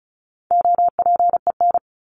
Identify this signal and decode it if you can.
Morse code, heard as a keyed tone, about 34 wpm, 700 Hz. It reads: OPEN
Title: Top 300 Words 35wpm